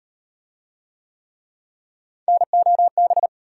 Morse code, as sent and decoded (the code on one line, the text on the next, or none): -. --- -...
NOB